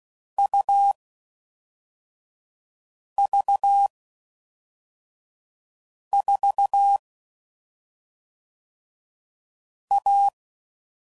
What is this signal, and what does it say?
Morse code, heard as a keyed tone, about 16 wpm, 800 Hz. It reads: U V 4 A
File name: Combinations / Difficult Combos / AUV AUV4